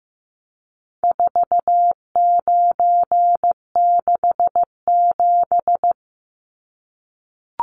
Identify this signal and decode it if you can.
Morse code, heard as a keyed tone, about 15 wpm, 700 Hz. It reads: 4967
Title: Numbers - 4-Digit 15wpm